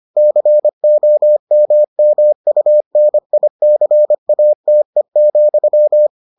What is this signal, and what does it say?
Morse code, heard as a keyed tone, about 25 wpm, 600 Hz. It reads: COMMUNICATE,